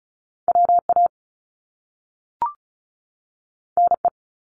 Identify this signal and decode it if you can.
Morse code, heard as a keyed tone, about 36 wpm, 700 Hz. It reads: WA NE
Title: US States 35wpm